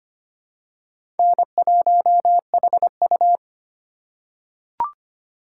Morse code, heard as a keyed tone, about 25 wpm, 700 Hz. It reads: N1HU
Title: USA Call Signs - 2x Character Spacing - 25wpm